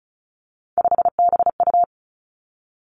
Morse code, heard as a keyed tone, about 36 wpm, 700 Hz. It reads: 5BU
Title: Three Challenging Letter-Number Combinations 35wpm